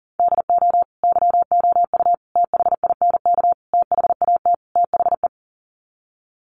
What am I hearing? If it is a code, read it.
Morse code, heard as a keyed tone, about 40 wpm, 700 Hz. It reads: DO YOU THINK THAT THE
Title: Sets of 5 Words Encore 40wpm